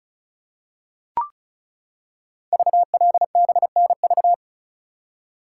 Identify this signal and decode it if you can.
Morse code, heard as a keyed tone, about 36 wpm, 690 Hz. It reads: VLBNV
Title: Code Groups - 5 Letter:Numbers - 35wpm